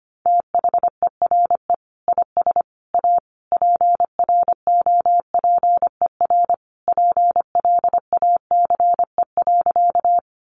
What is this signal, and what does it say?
Morse code, heard as a keyed tone, about 25 wpm, 700 Hz. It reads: THERE IS A PROPER PLACE.
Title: Sentences from Top 800 Words 25wpm